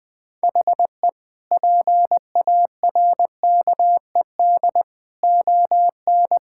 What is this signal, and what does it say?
Morse code, heard as a keyed tone, about 20 wpm, 700 Hz. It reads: HE PARKED ON
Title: Sentences with 4-Word Phrases - Spoken Beginning - 20wpm